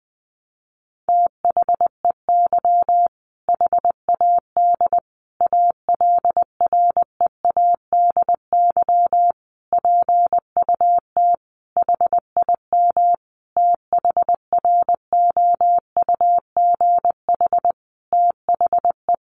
Morse code, heard as a keyed tone, about 20 wpm, 700 Hz. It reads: THEY HAD ALREADY PUT HIM THROUGH THE
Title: Sentences from Top 1200 Words 20wpm